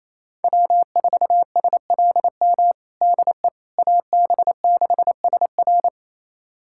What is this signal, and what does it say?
Morse code, heard as a keyed tone, about 28 wpm, 700 Hz. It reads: W4SLM DE AB6SR
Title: USA Callsign DE Callsign - 5 Characters - 28wpm